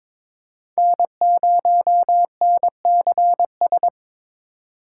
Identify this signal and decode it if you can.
Morse code, heard as a keyed tone, about 22 wpm, 700 Hz. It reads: N0NCS
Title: USA Call Signs - 5 Characters - 3x Character Spacing - 22wpm